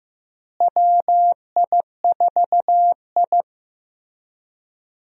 Morse code, heard as a keyed tone, about 15 wpm, 700 Hz. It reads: WI4I
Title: USA Call Signs - 4 Characters - Repeated 3x - 15wpm